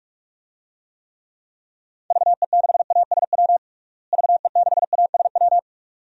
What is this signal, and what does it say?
Morse code, heard as a keyed tone, about 41 wpm, 700 Hz. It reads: VE6ASW VE6ASW
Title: DXCC-100 Call Signs - 6 Characters - Repeated 2x - 45wpm